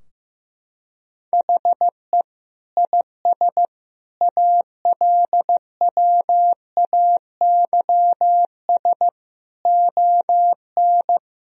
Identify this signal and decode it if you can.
Morse code, heard as a keyed tone, about 15 wpm, 700 Hz. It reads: HE IS ALWAYS ON
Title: Sentences from Top 1000 Words - Review Part 3 - 15wpm